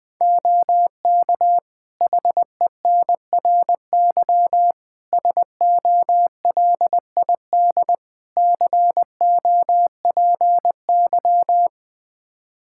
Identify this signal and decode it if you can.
Morse code, heard as a keyed tone, about 21 wpm, 700 Hz. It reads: OK HENRY SOLID COPY
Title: Practice QSO - Exchange 3 - More Info - 20wpm